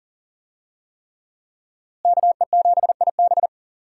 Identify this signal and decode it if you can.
Morse code, heard as a keyed tone, about 40 wpm, 700 Hz. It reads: KE7IB
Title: USA Call Signs - 5 Characters - Repeated 3x - 40wpm